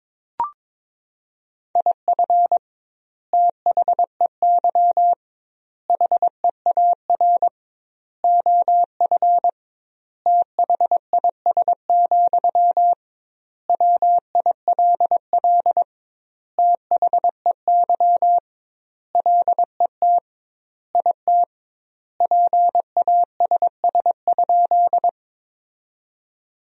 Morse code, heard as a keyed tone, about 22 wpm, 700 Hz. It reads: IF THEY HEAR OF THIS, WILL THEY LET IT PASS?
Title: 2x Word-Spacing - Sentences from Top 400 Words - 22wpm